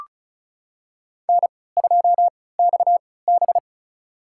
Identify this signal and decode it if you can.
Morse code, heard as a keyed tone, about 36 wpm, spaced as for 20 wpm, 700 Hz. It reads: N2XB
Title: USA Call Signs - 3x Character Spacing - 35wpm